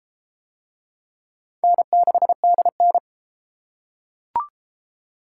Morse code, heard as a keyed tone, about 31 wpm, 700 Hz. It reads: N6DN